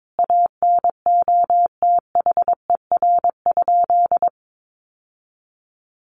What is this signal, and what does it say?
Morse code, heard as a keyed tone, about 22 wpm, 700 Hz. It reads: ANOTHER?
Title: Sentences from Top 600 Words 22wpm